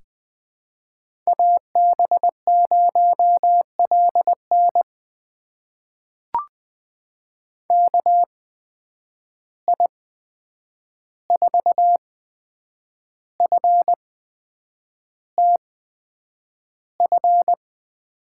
Morse code, heard as a keyed tone, about 20 wpm, spaced as for 8 wpm, 700 Hz. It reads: AB0LN KI4FTF
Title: USA Call Signs - 8x Character Spacing - 20wpm